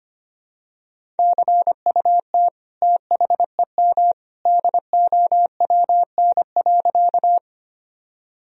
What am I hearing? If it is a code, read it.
Morse code, heard as a keyed tone, about 25 wpm, 700 Hz. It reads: CUT THEM DOWN.
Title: Sentences from Top 1000 Words - Review Part 1 - 25wpm